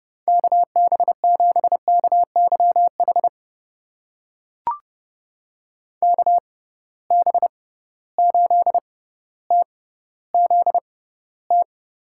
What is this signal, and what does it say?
Morse code, heard as a keyed tone, about 30 wpm, spaced as for 15 wpm, 700 Hz. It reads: KB7KYH KB8TZT
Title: USA Call Signs - 6 Characters - 6x Character Spacing - 30wpm